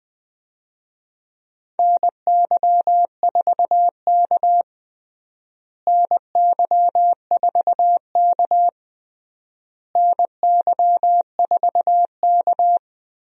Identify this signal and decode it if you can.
Morse code, heard as a keyed tone, about 20 wpm, 700 Hz. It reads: NY4K NY4K NY4K
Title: USA Call Signs - 4 Characters - Repeated 3x - 20wpm